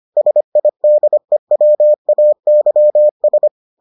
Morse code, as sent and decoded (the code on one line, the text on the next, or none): ... .. -.. .-- .- -.-- ...
SIDEWAYS